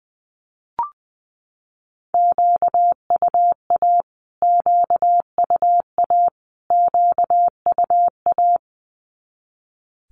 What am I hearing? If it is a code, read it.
Morse code, heard as a keyed tone, about 20 wpm, 700 Hz. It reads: QUA QUA QUA
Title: Q-Codes Repeated 3x 20wpm